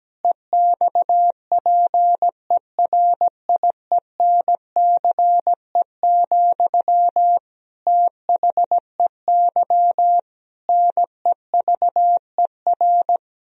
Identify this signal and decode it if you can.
Morse code, heard as a keyed tone, about 17 wpm, 700 Hz. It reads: EXPERIENCE, THEY NEVER
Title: Sentences from Top 1000 Words - Review Part 3 - 17wpm